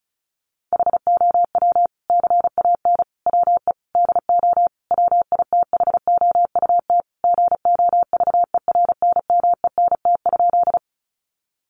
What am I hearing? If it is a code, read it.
Morse code, heard as a keyed tone, about 34 wpm, 700 Hz. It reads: HOW CAN WE DO WITHOUT GOVERNMENT?
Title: Sentences from Top 600 Words 35wpm